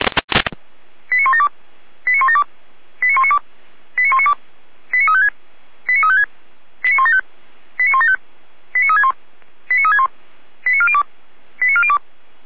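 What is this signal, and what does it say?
Five-tone selective call (ZVEI), heard as quick runs of five tones, tones of about 70 ms. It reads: 89172 89172 89192 89192 89356 89356 89175 89175 89361 89361 89402 89402
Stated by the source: ZVEI five-tone selective call received on an SDR.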